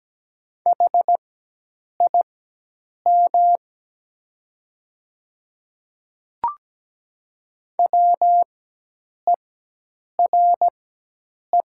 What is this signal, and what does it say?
Morse code, heard as a keyed tone, about 17 wpm, spaced as for 8 wpm, 700 Hz. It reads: HIM WERE
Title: 4x Character Spacing - Top 200 Words - 17wpm